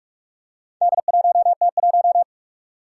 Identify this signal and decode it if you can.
Morse code, heard as a keyed tone, about 46 wpm, 700 Hz. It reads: D1T1
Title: Four Challenging Letter-Number Combinations 45wpm